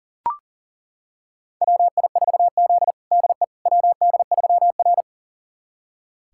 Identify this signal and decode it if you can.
Morse code, heard as a keyed tone, about 40 wpm, 700 Hz. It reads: WI4Z DE WD3R
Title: USA Callsign DE Callsign - 4 Characters - 40wpm